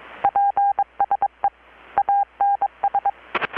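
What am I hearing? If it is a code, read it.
Morse code, heard as a keyed tone, about 21 wpm, 800 Hz, with radio noise. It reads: PSE ANS
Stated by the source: Morse from KPH's Night of Nights XX maritime radiotelegraphy event, received in Chiapas, Mexico about 3,800 km away on a Kenwood R-600 with a grounded 10 m long-wire antenna.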